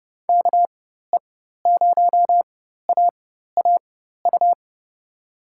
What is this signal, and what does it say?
Morse code, heard as a keyed tone, about 30 wpm, spaced as for 14 wpm, 700 Hz. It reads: KE0AAU
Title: USA Call Signs - 4x Character Spacing - 30wpm